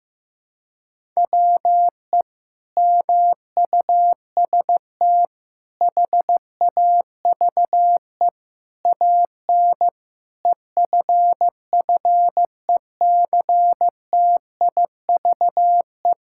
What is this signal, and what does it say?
Morse code, heard as a keyed tone, about 15 wpm, 700 Hz. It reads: WE MUST HAVE AN EFFECTIVE
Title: Sentences from Top 1000 Words - Review Part 6 - 15wpm